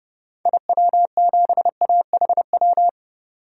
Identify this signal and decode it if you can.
Morse code, heard as a keyed tone, about 32 wpm, 700 Hz. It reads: IW7AHW
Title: DXCC-100 Call Signs - 6 Characters - 30wpm